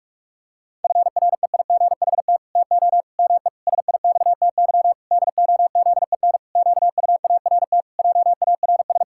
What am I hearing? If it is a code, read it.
Morse code, heard as a keyed tone, about 48 wpm, 700 Hz. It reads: FREIGHT TO ME SIXTY DOZEN QUART JARS
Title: Pangram Challenge v2 - 45wpm